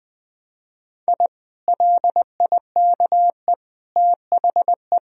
Morse code, heard as a keyed tone, about 20 wpm, 700 Hz. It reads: I LIKE THE
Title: Sentences from Top 1000 Words - Review Part 3 - 20wpm